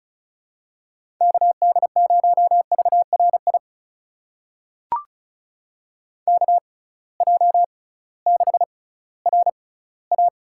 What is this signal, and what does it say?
Morse code, heard as a keyed tone, about 36 wpm, spaced as for 19 wpm, 700 Hz. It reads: KD0VRI KJ6RA
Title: USA Call Signs - 6 Characters - 6x Character Spacing - 35wpm